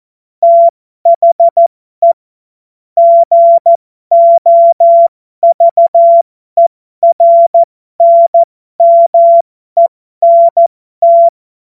Morse code, heard as a keyed tone, about 14 wpm, spaced as for 12 wpm, 680 Hz. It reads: THE GOVERNMENT